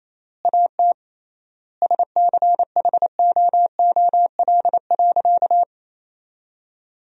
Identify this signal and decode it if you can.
Morse code, heard as a keyed tone, about 28 wpm, 700 Hz. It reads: AT SCHOOL.
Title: 3x Word-Spacing - Sentences from Top 300 Words - 28wpm